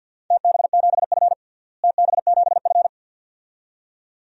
Morse code, heard as a keyed tone, about 52 wpm, 700 Hz. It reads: TB7F TB7F
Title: DXCC-100 Call Signs - 4 Characters - Repeated 2x - 50wpm